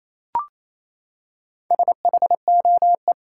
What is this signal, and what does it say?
Morse code, heard as a keyed tone, about 28 wpm, 700 Hz. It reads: SHOE
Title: Four Letter Words - 28wpm